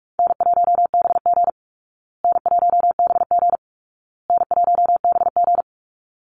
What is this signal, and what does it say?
Morse code, heard as a keyed tone, about 44 wpm, 700 Hz. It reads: N1BG N1BG N1BG
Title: Easy Call Signs Repeated 3x - 45wpm